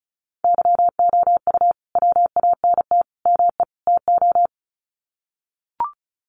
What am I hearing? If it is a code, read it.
Morse code, heard as a keyed tone, about 34 wpm, 700 Hz. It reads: YOU WANT ME TO
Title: Sets of 4 Words 35wpm